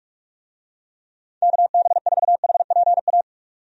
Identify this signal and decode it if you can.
Morse code, heard as a keyed tone, about 45 wpm, 700 Hz. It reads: KB4HPA
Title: USA Call Signs - 6 Characters - 45wpm